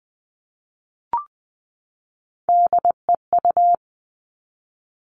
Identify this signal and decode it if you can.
Morse code, heard as a keyed tone, about 20 wpm, 700 Hz. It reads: DEU
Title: Three Challenging Letter-Number Combinations 20wpm